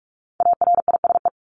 Morse code, heard as a keyed tone, about 44 wpm, 700 Hz. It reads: ARISE
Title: Top 1501-1750 Words 45wpm